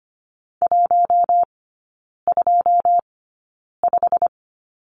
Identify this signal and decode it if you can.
Morse code, heard as a keyed tone, about 25 wpm, spaced as for 9 wpm, 690 Hz. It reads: 125
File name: Lesson 15e - Number 5 - Three Characters 25wpm